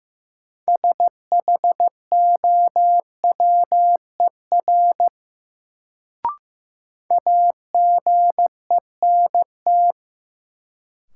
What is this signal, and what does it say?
Morse code, heard as a keyed tone, about 15 wpm, 700 Hz. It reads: SHOWER AGENT
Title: Sentences from Top 1600 Words - New Word Drill - 15wpm